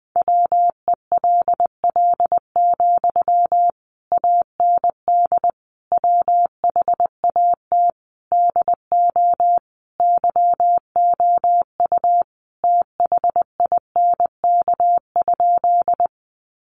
Morse code, heard as a keyed tone, about 20 wpm, 700 Hz. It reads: WELL, AND WHAT DO YOU THINK?
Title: Sentences from Top 200 Words 20wpm